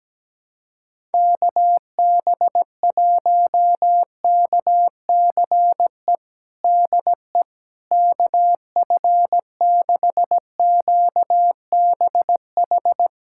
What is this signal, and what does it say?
Morse code, heard as a keyed tone, about 17 wpm, 700 Hz. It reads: KB1KCE DE KF6QBH
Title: USA Callsign DE Callsign 17wpm